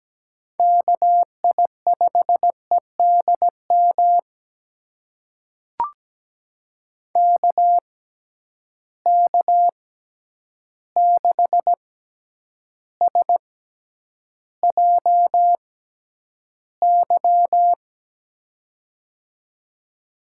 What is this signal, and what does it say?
Morse code, heard as a keyed tone, about 17 wpm, spaced as for 9 wpm, 700 Hz. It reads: KI5EDM KK6SJY
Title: USA Call Signs - 6 Characters - 6x Character Spacing - 17wpm